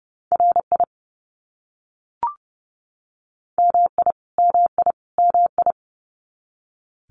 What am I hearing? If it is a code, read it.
Morse code, heard as a keyed tone, about 30 wpm, 700 Hz. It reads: RI MI MI MI
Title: US States Repeated 3x 30wpm